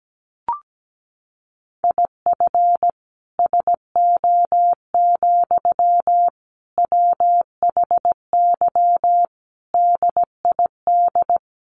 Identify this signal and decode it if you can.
Morse code, heard as a keyed tone, about 17 wpm, 700 Hz. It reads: IF SO, WHY DID
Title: Sentences from Top 1400 Words 17wpm